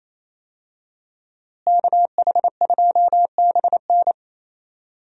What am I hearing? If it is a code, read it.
Morse code, heard as a keyed tone, about 28 wpm, 700 Hz. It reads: KH2BN
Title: USA Call Signs - 5 Characters - Repeated 2x - 28wpm